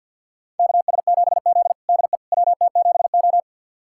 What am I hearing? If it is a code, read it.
Morse code, heard as a keyed tone, about 50 wpm, 700 Hz. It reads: KS7Z DE WT7O
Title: USA Callsign DE Callsign - 4 Characters - 50wpm